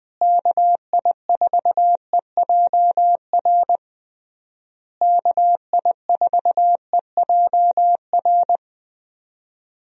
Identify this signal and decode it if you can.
Morse code, heard as a keyed tone, about 20 wpm, 700 Hz. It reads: KI4EJR KI4EJR
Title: USA Call Signs - 6 Characters - Repeated 3x - 20wpm